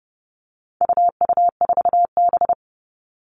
Morse code, heard as a keyed tone, about 30 wpm, 700 Hz. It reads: UU4B
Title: All Cut Numbers - 4-Digit 30wpm